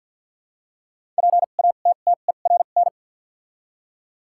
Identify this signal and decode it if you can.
Morse code, heard as a keyed tone, about 53 wpm, 700 Hz. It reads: P A T T E R N
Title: 2x Character Spacing - Top 500 Words - 50wpm